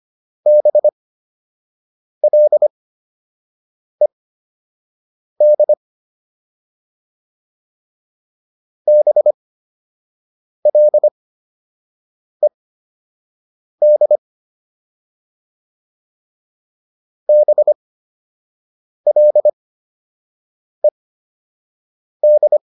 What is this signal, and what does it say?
Morse code, heard as a keyed tone, about 25 wpm, spaced as for 6 wpm, 600 Hz. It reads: BLED BLED BLED